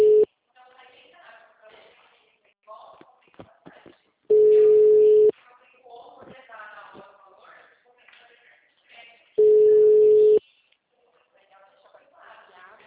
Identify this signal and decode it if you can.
Ringback tone (Europe, 425 Hz), with speech.
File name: out- -2003- - - .57478